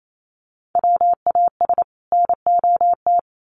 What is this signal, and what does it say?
Morse code, heard as a keyed tone, about 28 wpm, 700 Hz. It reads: WAS NOT
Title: Sets of 2 Words 28wpm